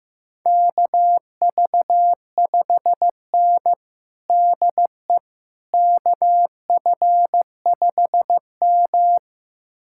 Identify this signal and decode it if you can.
Morse code, heard as a keyed tone, about 15 wpm, 700 Hz. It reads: KV5N DE KF5M
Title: USA Callsign DE Callsign - 4 Characters - 15wpm